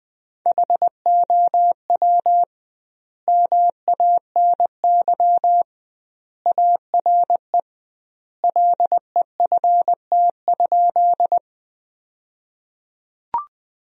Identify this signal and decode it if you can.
Morse code, heard as a keyed tone, about 20 wpm, 700 Hz. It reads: HOW MANY ARE LEFT?
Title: 2x Word-Spacing - Sentences from Top 300 Words - 20wpm